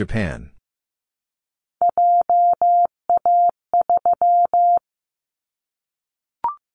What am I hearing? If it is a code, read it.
Morse code, heard as a keyed tone, about 14 wpm, 700 Hz. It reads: JA3
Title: Callsign Prefixes for Top 25 DXCC - 15wpm